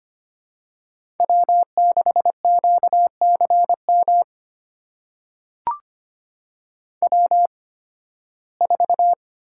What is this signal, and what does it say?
Morse code, heard as a keyed tone, about 25 wpm, spaced as for 16 wpm, 700 Hz. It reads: W6QCM W4
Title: USA Call Signs - 8x Character Spacing - 25wpm